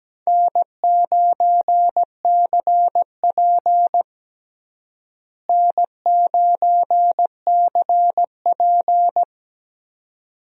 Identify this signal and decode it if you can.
Morse code, heard as a keyed tone, about 17 wpm, 700 Hz. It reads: N9CP N9CP
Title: USA Call Signs - 4 Characters - Repeated 2x - 17wpm